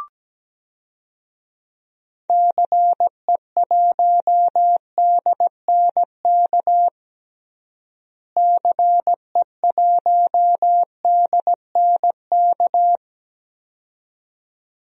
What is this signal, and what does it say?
Morse code, heard as a keyed tone, about 17 wpm, 700 Hz. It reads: CE1DNK CE1DNK
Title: DXCC-100 Call Signs - 6 Characters - Repeated 2x - 17wpm